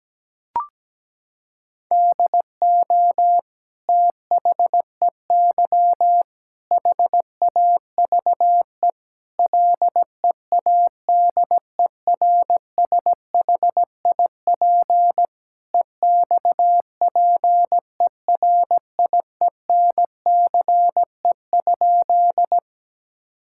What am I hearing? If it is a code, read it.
Morse code, heard as a keyed tone, about 17 wpm, 700 Hz. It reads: DO THEY HAVE LEADERSHIP EXPERIENCE?
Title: Sentences from Top 1600 Words 17wpm